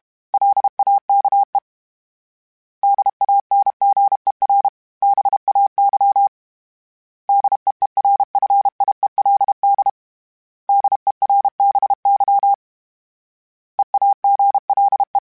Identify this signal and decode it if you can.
Morse code, heard as a keyed tone, about 32 wpm, 800 Hz. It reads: LAKE DANGER BAY DEERFIELD DERBY EAGLE
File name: fromSMs / FromSMs mp3s / Cities@30